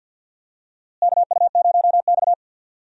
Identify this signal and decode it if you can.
Morse code, heard as a keyed tone, about 48 wpm, 700 Hz. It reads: KU0X
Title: USA Call Signs - 4 Characters - Repeated 2x - 50wpm